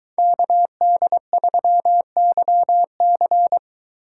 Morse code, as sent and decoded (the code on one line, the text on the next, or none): -.- -.. ...-- -.-- -.-.
KD3YC